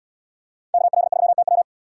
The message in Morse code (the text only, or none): KL3TW